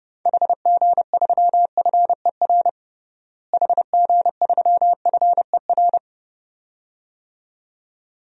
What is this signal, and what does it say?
Morse code, heard as a keyed tone, about 29 wpm, 700 Hz. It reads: HG3FER HG3FER
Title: DXCC-100 Call Signs - 6 Characters - Repeated 2x - 30wpm